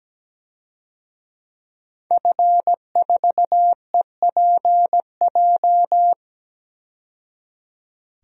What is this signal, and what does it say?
Morse code, heard as a keyed tone, about 17 wpm, 700 Hz. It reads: F4EPJ
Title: DXCC-100 Call Signs - 5 Characters - 17wpm